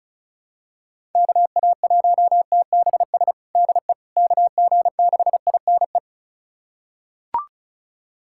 Morse code, heard as a keyed tone, about 34 wpm, 700 Hz. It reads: KA1TBS DE KG6INE
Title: USA Callsign DE Callsign 35wpm